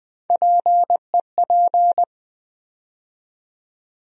Morse code, heard as a keyed tone, about 20 wpm, 700 Hz. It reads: PEP
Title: USA Call Sign Postfixes - 3 Letters 20wpm